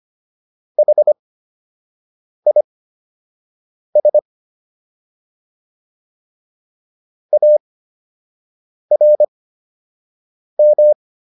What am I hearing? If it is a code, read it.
Morse code, heard as a keyed tone, about 25 wpm, spaced as for 6 wpm, 600 Hz. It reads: HIS ARM